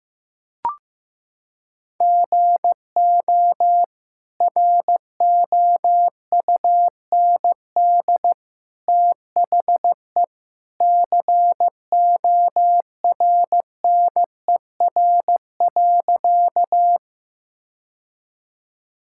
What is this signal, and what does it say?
Morse code, heard as a keyed tone, about 15 wpm, 700 Hz. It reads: GO ROUND THE CORNER.
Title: Sentences from Top 1600 Words 15wpm